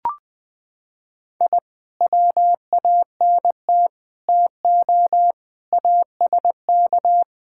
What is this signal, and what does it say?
Morse code, heard as a keyed tone, about 20 wpm, 700 Hz. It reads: I WANT TO ASK
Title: Sentences with 5-Word Phrases - 20wpm